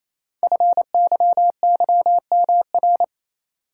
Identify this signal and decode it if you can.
Morse code, heard as a keyed tone, about 27 wpm, 700 Hz. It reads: FYYMR
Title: Code Groups - 5 Letter:Numbers - 28wpm